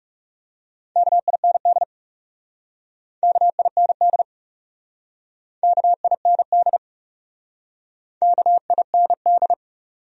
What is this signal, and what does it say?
Morse code, heard as a keyed tone, about 38 wpm, 700 Hz. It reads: KIND KIND KIND KIND